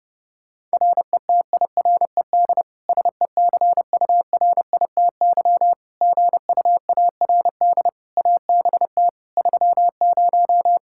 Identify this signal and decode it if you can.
Morse code, heard as a keyed tone, about 32 wpm, 700 Hz. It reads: RETIRED SECURITY GUARD ABT 30
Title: K1USN SST QSO Practice 30wpm